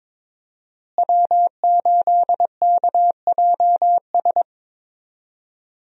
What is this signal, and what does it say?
Morse code, heard as a keyed tone, about 21 wpm, 700 Hz. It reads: W8KJS
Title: USA Call Signs - 5 Characters - Repeated 3x - 22wpm